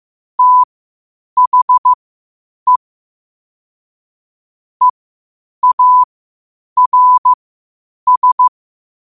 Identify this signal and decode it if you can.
Morse code, heard as a keyed tone, about 15 wpm, spaced as for 8 wpm, 1000 Hz. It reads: THE EARS
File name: qotdfiles / 8wpm